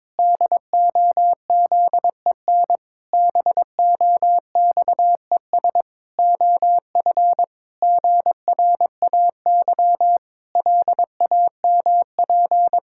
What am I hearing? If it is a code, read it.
Morse code, heard as a keyed tone, about 22 wpm, 700 Hz. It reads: DOZEN BOXES OF GRAY LAMP
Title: Pangram Challenge v2 - 22wpm